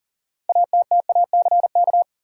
Morse code, heard as a keyed tone, about 37 wpm, 700 Hz. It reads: ATTACK
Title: Six Letter Words - 40wpm